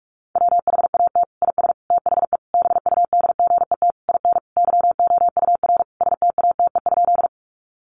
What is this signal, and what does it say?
Morse code, heard as a keyed tone, about 44 wpm, 700 Hz. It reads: WHAT IS THE BUDGET IN YOUR STATE?
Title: Sentences from Top 1200 Words 45wpm